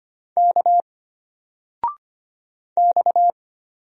Morse code, heard as a keyed tone, about 25 wpm, 700 Hz. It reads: K X
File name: Single-Letter-Number-25wpm